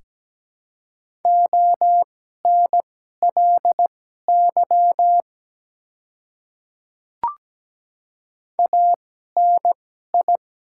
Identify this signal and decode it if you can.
Morse code, heard as a keyed tone, about 17 wpm, spaced as for 12 wpm, 700 Hz. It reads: ONLY ANI
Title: 2x Character Spacing - Top 200 Words - 17wpm